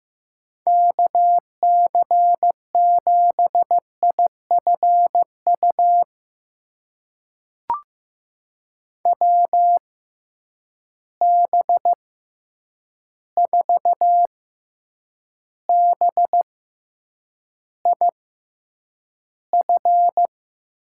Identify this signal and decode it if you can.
Morse code, heard as a keyed tone, about 15 wpm, spaced as for 8 wpm, 700 Hz. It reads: KC7IFU WB4BIF
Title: USA Call Signs - 6 Characters - 6x Character Spacing - 15wpm